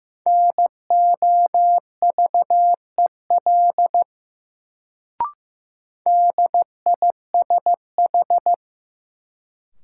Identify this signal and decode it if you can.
Morse code, heard as a keyed tone, about 16 wpm, 700 Hz. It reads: NOVEL DISH
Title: Sentences from Top 1300 Words - New Word Drill Alt - 15wpm